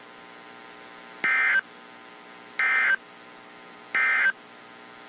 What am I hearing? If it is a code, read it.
Emergency Alert System (SAME) tones, heard as harsh buzzing bursts. It is heard from 1.1 to 4.4 s.
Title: WZ2523 - NOAA Weather Radio